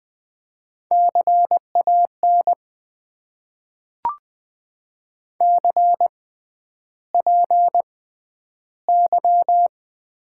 Morse code, heard as a keyed tone, about 20 wpm, spaced as for 10 wpm, 700 Hz. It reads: CAN CPY
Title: Top 90 QSO Elements - 6x Character Spacing - 20wpm